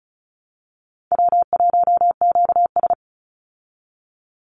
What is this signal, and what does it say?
Morse code, heard as a keyed tone, about 34 wpm, 700 Hz. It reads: W1QS
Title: Easy Call Signs Repeated 3x - 35wpm